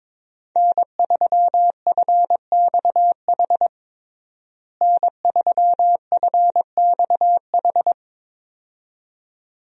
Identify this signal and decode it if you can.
Morse code, heard as a keyed tone, about 22 wpm, 700 Hz. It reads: N3FXH N3FXH
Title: USA Call Signs - 5 Characters - Repeated 2x - 22wpm